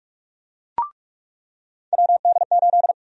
Morse code, heard as a keyed tone, about 46 wpm, 700 Hz. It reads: WD8
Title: USA Call Sign Prefixes - 45wpm